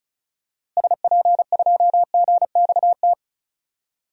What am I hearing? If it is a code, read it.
Morse code, heard as a keyed tone, about 36 wpm, 700 Hz. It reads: SP2GXT
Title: DXCC-100 Call Signs - 6 Characters - Repeated 2x - 35wpm